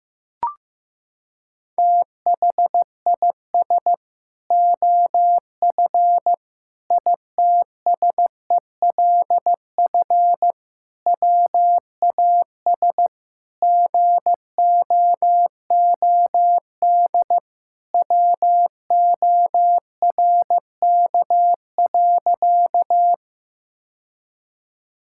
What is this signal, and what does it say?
Morse code, heard as a keyed tone, about 16 wpm, 700 Hz. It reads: THIS OF ITSELF WAS GOOD WORK.
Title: Sentences from Top 1000 Words - Review Part 5 - 15wpm